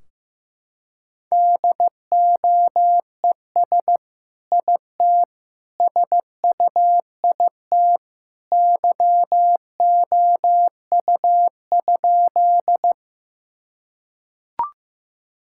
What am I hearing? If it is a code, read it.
Morse code, heard as a keyed tone, about 15 wpm, 700 Hz. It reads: DOES IT SUIT YOU?
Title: Sentences from Top 1600 Words 15wpm